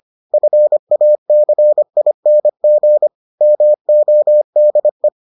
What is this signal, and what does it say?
Morse code, heard as a keyed tone, about 25 wpm, 600 Hz. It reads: FACING MODE